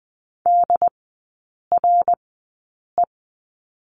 Morse code, heard as a keyed tone, about 20 wpm, spaced as for 8 wpm, 700 Hz. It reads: DRE
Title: 114 Lesson 19e - Letter w - Three Characters 20wpm